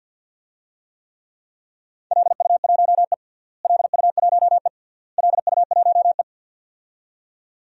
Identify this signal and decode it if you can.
Morse code, heard as a keyed tone, about 51 wpm, 700 Hz. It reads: LU1E LU1E LU1E
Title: DXCC-100 Call Signs - 4 Characters - Repeated 3x - 50wpm